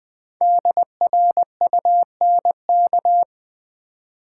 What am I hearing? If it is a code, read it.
Morse code, heard as a keyed tone, about 20 wpm, 700 Hz. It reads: DRUNK
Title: Build A Word - 1 Letter At A Time - 5-Letter Words - 20wpm